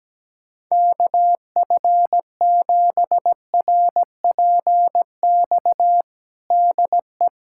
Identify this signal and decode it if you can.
Morse code, heard as a keyed tone, about 17 wpm, 700 Hz. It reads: KF7RPX DE
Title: USA Callsign DE Callsign - 6 Characters - 17wpm